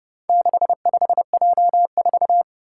Morse code, heard as a keyed tone, about 30 wpm, 700 Hz. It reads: 65J4
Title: Four Challenging Letter-Number Combinations 30wpm